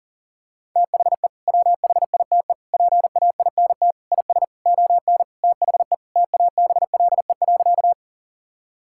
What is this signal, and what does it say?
Morse code, heard as a keyed tone, about 40 wpm, 700 Hz. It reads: THE WHITE PAINT IS ON THE TABLE.
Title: Sentences from Top 1000 Words - Review Part 6 - 40wpm